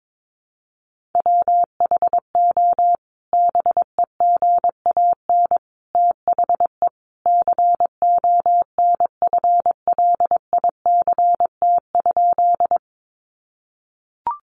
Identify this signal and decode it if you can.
Morse code, heard as a keyed tone, about 22 wpm, 700 Hz. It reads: WHO BEGAN THE CONFLICT?
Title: Sentences from Top 1300 Words 22wpm